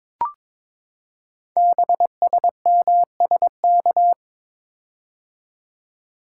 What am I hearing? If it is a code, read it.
Morse code, heard as a keyed tone, about 22 wpm, 700 Hz. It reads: BSMSK
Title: Code Groups - 5 Letters - 22wpm